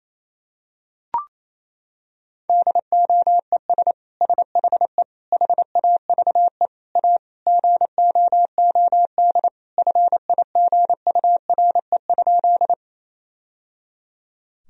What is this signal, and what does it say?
Morse code, heard as a keyed tone, about 28 wpm, 700 Hz. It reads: DOES SHE HAVE A GOOD FIGURE?
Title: Sentences from Top 1000 Words - Review Part 2 - 28wpm